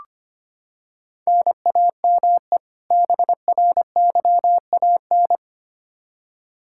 Morse code, heard as a keyed tone, about 25 wpm, 700 Hz. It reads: NAME BRYAN
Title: K1USN SST QSO Practice 25wpm